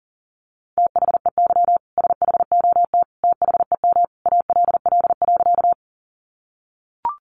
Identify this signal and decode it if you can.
Morse code, heard as a keyed tone, about 40 wpm, 700 Hz. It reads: THEY SHOT THEM ALL.
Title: Sentences from Top 1100 Words 40wpm